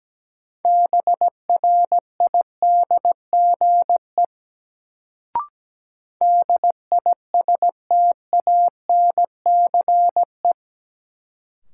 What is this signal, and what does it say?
Morse code, heard as a keyed tone, about 17 wpm, 700 Hz. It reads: BRIDGE DISTANCE
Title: Sentences from Top 1200 Words - New Word Drill Alt - 17wpm